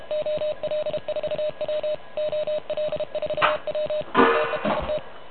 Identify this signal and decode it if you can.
Morse code, heard as a keyed tone, about 30 wpm, 600 Hz, with noise. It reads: OL4W OL4W TEST
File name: truSDX / MWC29.8